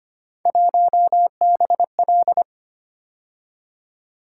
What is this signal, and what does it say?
Morse code, heard as a keyed tone, about 25 wpm, 700 Hz. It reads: 1BL
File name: Three Letter-Number Combinations - 25wpm